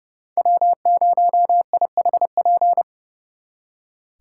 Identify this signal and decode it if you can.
Morse code, heard as a keyed tone, about 35 wpm, 700 Hz. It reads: W0IHP
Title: Speed Racing - USA 1x3 Call Signs 30wpm